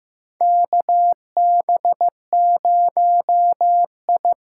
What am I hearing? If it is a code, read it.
Morse code, heard as a keyed tone, about 15 wpm, 700 Hz. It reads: KB0I
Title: USA Call Signs - 6 Characters - 6x Character Spacing - 15wpm